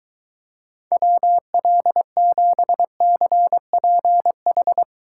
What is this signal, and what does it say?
Morse code, heard as a keyed tone, about 23 wpm, 700 Hz. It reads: WL7CPH